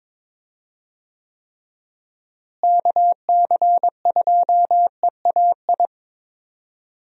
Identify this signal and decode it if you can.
Morse code, heard as a keyed tone, about 22 wpm, 700 Hz. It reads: KC2EAI
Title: USA Call Signs - 6 Characters - Repeated 2x - 22wpm